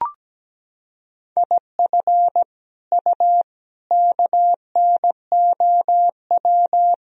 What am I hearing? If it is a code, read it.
Morse code, heard as a keyed tone, about 17 wpm, 700 Hz. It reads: IF U KNOW
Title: Sentences from Top 400 Words and Abbreviations 17wpm